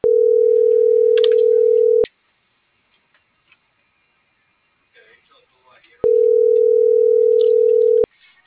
Ringback tone (North America), 440 + 480 Hz.
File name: CENTRAL MABE / out xquenda.trinidad - .12095